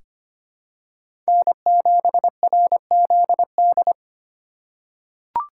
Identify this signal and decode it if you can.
Morse code, heard as a keyed tone, about 25 wpm, 700 Hz. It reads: N7RZD
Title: USA Call Signs - 5 Characters - 3x Character Spacing - 25wpm